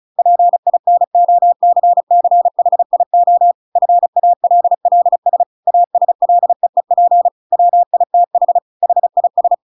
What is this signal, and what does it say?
Morse code, heard as a keyed tone, about 34 wpm, 700 Hz. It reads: PINOCCHIO FALLS ASLEEP WITH HIS
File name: Pinocchio / Pinocchio ch6 35 35 ik1hge